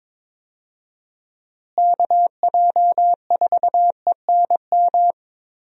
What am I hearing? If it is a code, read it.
Morse code, heard as a keyed tone, about 22 wpm, 700 Hz. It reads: KJ4ENM
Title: USA Call Signs - 6 Characters - Repeated 2x - 22wpm